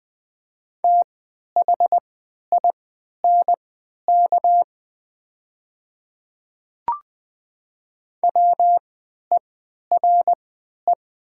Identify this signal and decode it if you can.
Morse code, heard as a keyed tone, about 20 wpm, spaced as for 11 wpm, 700 Hz. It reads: THINK WERE
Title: 3x Character Spacing - Top 200 Words - 20wpm